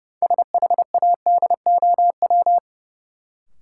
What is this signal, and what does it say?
Morse code, heard as a keyed tone, about 30 wpm, 700 Hz. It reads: SHADOW
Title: Sets of 5 Words Encore - New Word Drill - 30wpm